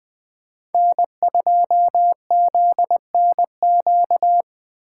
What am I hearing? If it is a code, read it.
Morse code, heard as a keyed tone, about 20 wpm, 700 Hz. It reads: N2ZNQ
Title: Speed Racing - USA Call Signs 17wpm